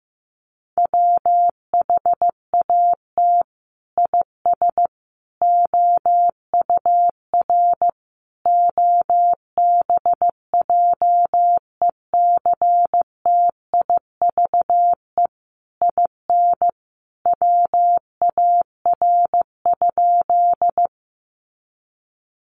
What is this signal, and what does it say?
Morse code, heard as a keyed tone, about 15 wpm, 700 Hz. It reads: WHAT IS OUR OBJECTIVE IN WAR?
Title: Sentences from Top 1200 Words 15wpm